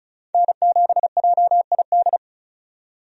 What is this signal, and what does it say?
Morse code, heard as a keyed tone, about 34 wpm, 700 Hz. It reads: N7JID
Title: USA Call Signs - 5 Characters - 35wpm